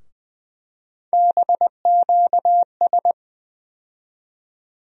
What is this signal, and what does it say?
Morse code, heard as a keyed tone, about 20 wpm, 700 Hz. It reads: BQS
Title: USA Call Sign Postfixes - 3 Letters 20wpm